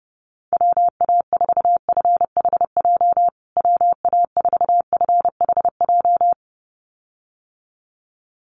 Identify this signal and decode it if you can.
Morse code, heard as a keyed tone, about 30 wpm, 700 Hz. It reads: WA4FHJ WA4FHJ
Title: USA Call Signs Repeated 2x 30wpm